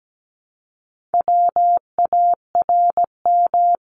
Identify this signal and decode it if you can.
Morse code, heard as a keyed tone, about 17 wpm, 700 Hz. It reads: WARM
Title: Top 500 Words Repeated 17wpm